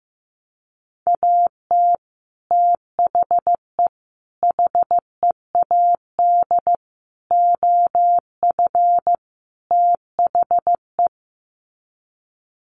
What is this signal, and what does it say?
Morse code, heard as a keyed tone, about 15 wpm, 700 Hz. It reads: AT THE HEAD OF THE
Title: Sets of 5 Words 15wpm